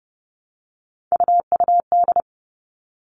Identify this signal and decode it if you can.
Morse code, heard as a keyed tone, about 30 wpm, 700 Hz. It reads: UUD
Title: All Cut Numbers - 3-Digit 30wpm